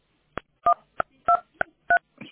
DTMF, heard as telephone touch tones, tones of about 90 ms. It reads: 123